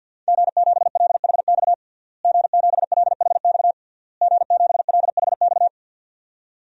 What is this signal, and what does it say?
Morse code, heard as a keyed tone, about 46 wpm, 700 Hz. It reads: G7LHX G7LHX G7LHX
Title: DXCC-100 Call Signs - 5 Characters - Repeated 3x - 50wpm